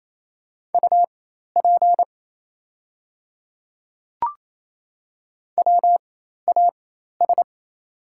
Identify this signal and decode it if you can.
Morse code, heard as a keyed tone, about 27 wpm, spaced as for 13 wpm, 690 Hz. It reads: UP WAS
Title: 4x Character Spacing - Top 100 Words - 28wpm